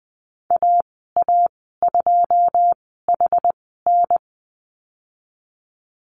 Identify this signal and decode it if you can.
Morse code, heard as a keyed tone, about 20 wpm, spaced as for 14 wpm, 700 Hz. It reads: AA2HN
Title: 082 Lesson 14g - Number 2 - Call Signs 20wpm